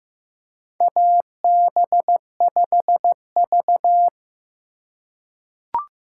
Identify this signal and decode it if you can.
Morse code, heard as a keyed tone, about 16 wpm, 700 Hz. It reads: AB5V
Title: USA Call Signs - 4 Characters - 4x Character Spacing - 15wpm